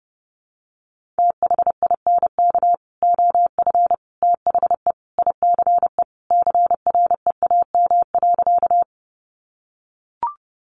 Morse code, heard as a keyed tone, about 30 wpm, 700 Hz. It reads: THINK OF THE ICE CREAM.
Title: Sentences from Top 1500 Words 30wpm